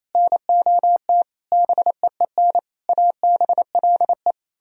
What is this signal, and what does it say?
Morse code, heard as a keyed tone, about 34 wpm, 700 Hz. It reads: NOT BEEN ABLE
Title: Speed-Racing - Sets of 5 Words 28wpm